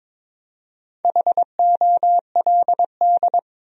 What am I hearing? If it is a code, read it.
Morse code, heard as a keyed tone, about 21 wpm, 700 Hz. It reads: HOLD
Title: 4x Word-Spacing - Sentences from Top 400 Words - 22wpm